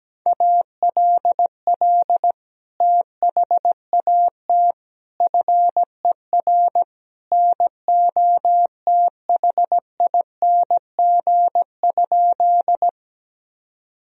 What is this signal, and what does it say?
Morse code, heard as a keyed tone, about 17 wpm, 700 Hz. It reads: ALL THAT FER NOTHING?
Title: Sentences from Top 500 Words and Abbreviations 17wpm